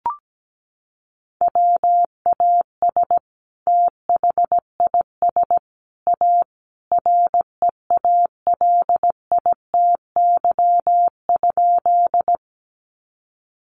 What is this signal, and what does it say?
Morse code, heard as a keyed tone, about 17 wpm, 700 Hz. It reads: WAS THIS A REALITY?
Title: Sentences from Top 1200 Words 17wpm